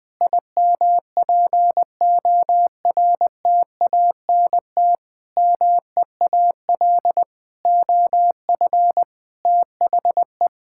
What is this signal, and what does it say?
Morse code, heard as a keyed tone, about 20 wpm, 700 Hz. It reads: IMPORTANT MEAL OF THE
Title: Sentences with 9-Word Phrases - 20wpm